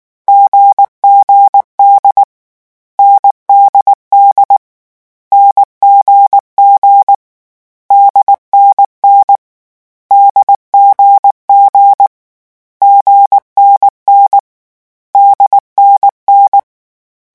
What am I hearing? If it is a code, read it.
Morse code, heard as a keyed tone, about 19 wpm, 800 Hz. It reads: GGD NDD NGG DNN DGG GNN DNN
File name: morse mp3 / lecon4-vitesse 18